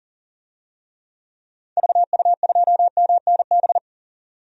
Morse code, heard as a keyed tone, about 37 wpm, 700 Hz. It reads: VU2MNB
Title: DXCC-100 Call Signs - 6 Characters - 40wpm